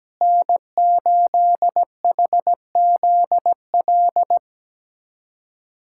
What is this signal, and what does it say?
Morse code, heard as a keyed tone, about 17 wpm, 700 Hz. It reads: N8HZL
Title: USA Callsign DE Callsign - 5 Characters - 17wpm